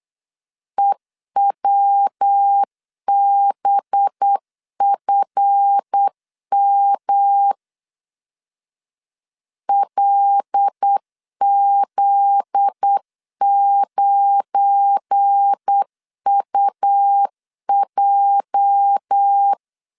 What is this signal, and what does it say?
Morse code, heard as a keyed tone, about 8 wpm, 800 Hz. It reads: EWBFM LZ9UJ
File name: LEZIONI / 8 YJ45 / 8 YJ 8wpm 3313